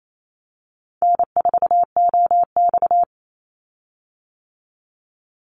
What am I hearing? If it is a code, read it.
Morse code, heard as a keyed tone, about 28 wpm, 700 Hz. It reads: N4OX
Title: Easy Call Signs - 28wpm